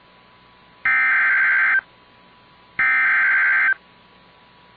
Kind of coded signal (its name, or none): Emergency Alert System (SAME) tones